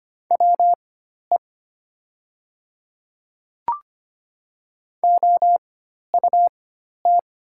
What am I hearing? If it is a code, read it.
Morse code, heard as a keyed tone, about 25 wpm, spaced as for 12 wpm, 700 Hz. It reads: WE OUT
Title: 4x Character Spacing - Top 100 Words - 25wpm